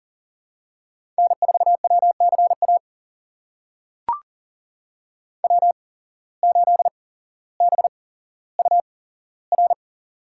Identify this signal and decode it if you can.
Morse code, heard as a keyed tone, about 40 wpm, spaced as for 17 wpm, 700 Hz. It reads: N4WCA W8BUR
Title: USA Call Signs - 5 Characters - 8x Character Spacing - 40wpm